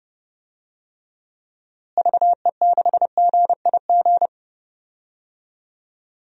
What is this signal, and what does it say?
Morse code, heard as a keyed tone, about 32 wpm, 700 Hz. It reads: VE6GIG
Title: DXCC-100 Call Signs - 6 Characters - Repeated 3x - 30wpm